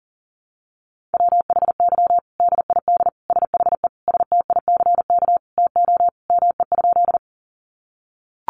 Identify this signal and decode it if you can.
Morse code, heard as a keyed tone, about 40 wpm, 700 Hz. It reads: WHY DID SHE STICK TO ME?
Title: Sentences from Top 1200 Words 40wpm